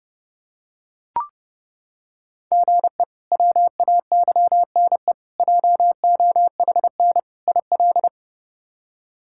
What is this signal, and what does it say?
Morse code, heard as a keyed tone, about 32 wpm, 700 Hz. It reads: GE WAYNE JOHN IL
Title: K1USN SST QSO Practice - 30wpm